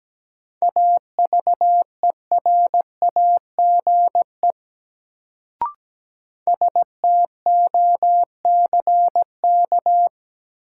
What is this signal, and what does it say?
Morse code, heard as a keyed tone, about 17 wpm, 700 Hz. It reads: AVERAGE STOCK
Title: Sets of 5 Words - New Word Drill - 17wpm